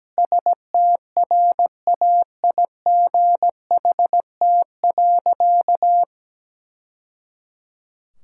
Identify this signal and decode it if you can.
Morse code, heard as a keyed tone, about 17 wpm, 700 Hz. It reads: STRAIGHT.
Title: Sentences from Top 1000 Words - Review Part 6 - 17wpm